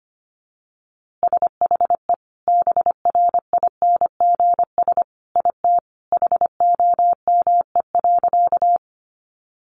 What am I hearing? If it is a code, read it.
Morse code, heard as a keyed tone, about 25 wpm, 700 Hz. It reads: SHE BRINGS IT HOME.
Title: Easy Phrases 25wpm